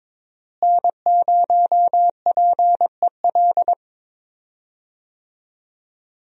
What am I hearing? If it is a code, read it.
Morse code, heard as a keyed tone, about 22 wpm, 700 Hz. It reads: N0PEL
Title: USA Call Signs - 5 Characters - Repeated 3x - 22wpm